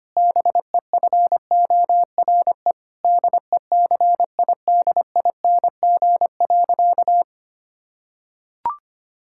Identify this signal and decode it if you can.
Morse code, heard as a keyed tone, about 25 wpm, 700 Hz. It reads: BEFORE DECIDING.
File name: practice / 25wpm / Difficult Sentences with Hints - 25wpm